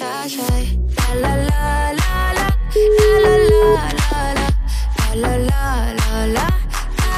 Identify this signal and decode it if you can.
Ringback tone (Europe, 425 Hz), with noise.